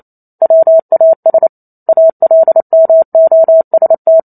Morse code, heard as a keyed tone, about 28 wpm, 650 Hz. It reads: WAS ALMOST